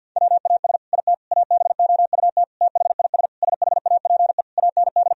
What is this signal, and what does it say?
Morse code, heard as a keyed tone, about 50 wpm, 700 Hz. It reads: WAS IT ABOUT THIS SHAPE AND
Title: Sentences from Top 1000 Words - Review Part 3 - 50wpm